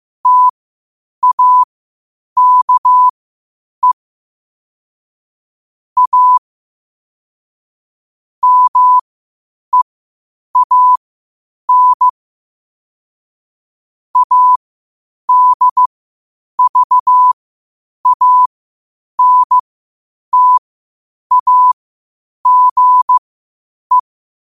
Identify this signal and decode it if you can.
Morse code, heard as a keyed tone, about 15 wpm, spaced as for 8 wpm, 1000 Hz. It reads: TAKE A MEAN ADVANTAGE
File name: qotdfiles / 8wpm